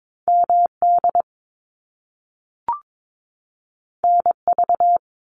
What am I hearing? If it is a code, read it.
Morse code, heard as a keyed tone, about 22 wpm, 700 Hz. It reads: MD NV
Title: US States 22wpm